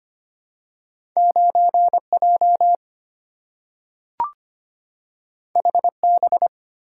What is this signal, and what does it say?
Morse code, heard as a keyed tone, about 25 wpm, 700 Hz. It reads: 9J HB
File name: Two Letter-Number Combinations - 25wpm